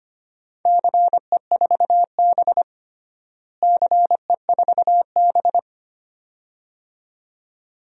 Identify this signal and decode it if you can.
Morse code, heard as a keyed tone, about 25 wpm, 700 Hz. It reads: CE4B CE4B
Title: DXCC-100 Call Signs - 4 Characters - Repeated 2x - 25wpm